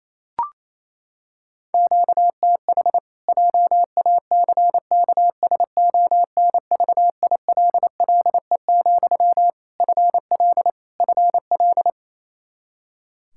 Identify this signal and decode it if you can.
Morse code, heard as a keyed tone, about 28 wpm, 700 Hz. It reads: QTH JACKSONVILLE, FL FL
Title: Practice QSO - Exchange 1 - Beginning - 28wpm